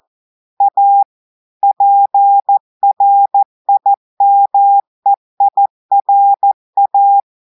Morse code, heard as a keyed tone, about 14 wpm, 800 Hz. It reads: A PRIMEIRA
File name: audios / portugais / portugais texte en clair 1 vitesse 14 mots minutes